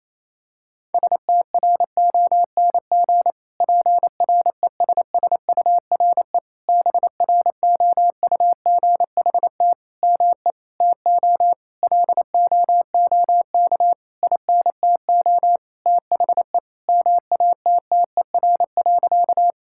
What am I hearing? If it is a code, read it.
Morse code, heard as a keyed tone, about 28 wpm, 700 Hz. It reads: STRONG PRESSURE BROUGHT ME TO LOOK INTO THE MATTER.
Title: Sentences from Top 1000 Words - Review Part 6 - 28wpm